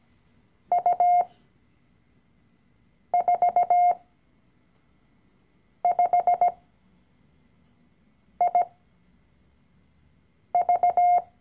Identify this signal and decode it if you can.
Morse code, heard as a keyed tone, about 17 wpm, spaced as for 4 wpm, 710 Hz. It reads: U45IV
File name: CW-Online-Kurs / Anfänger / Lektion 012